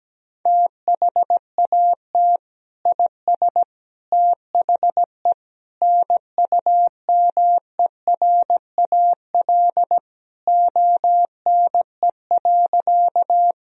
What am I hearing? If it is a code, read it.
Morse code, heard as a keyed tone, about 17 wpm, 700 Hz. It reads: THAT IS THE NUMERAL ONE.
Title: Sentences from Top 1000 Words - Review Part 2 - 17wpm